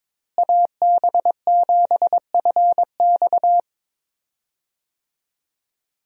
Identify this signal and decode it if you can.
Morse code, heard as a keyed tone, about 22 wpm, 700 Hz. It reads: AB7FX
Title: USA Call Signs - 5 Characters - Repeated 3x - 22wpm